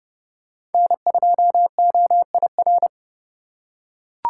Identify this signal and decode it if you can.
Morse code, heard as a keyed tone, about 30 wpm, 700 Hz. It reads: N2OIR
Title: USA Call Signs - 5 Characters - 4x Character Spacing - 30wpm